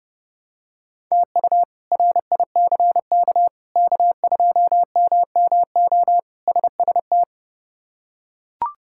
Morse code, heard as a keyed tone, about 30 wpm, 700 Hz. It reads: TU RICK K2MMO SST
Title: K1USN SST QSO Practice - 30wpm